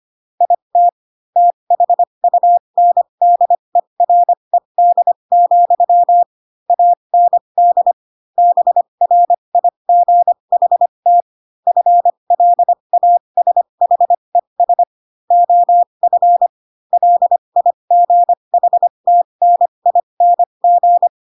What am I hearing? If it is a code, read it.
Morse code, heard as a keyed tone, about 25 wpm, spaced as for 22 wpm, 700 Hz. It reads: IT THUNDERED, AND BRIGHT FLASHES OF LIGHTNING
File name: Pinocchio / Pinocchio ch6 18 25 ik1hge